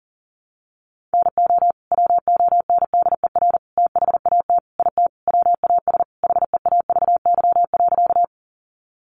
Morse code, heard as a keyed tone, about 40 wpm, 700 Hz. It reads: NO WONDER THAT IT WAS HEAVY.
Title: Sentences from Top 1100 Words 40wpm